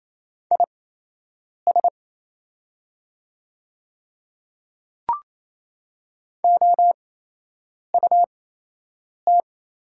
Morse code, heard as a keyed tone, about 27 wpm, spaced as for 8 wpm, 700 Hz. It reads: IS OUT
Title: 8x Character Spacing - Top 200 Words - 28wpm